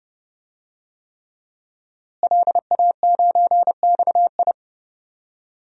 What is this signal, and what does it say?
Morse code, heard as a keyed tone, about 30 wpm, 700 Hz. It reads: LA9XI
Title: DXCC-100 Call Signs - 5 Characters - Repeated 3x - 30wpm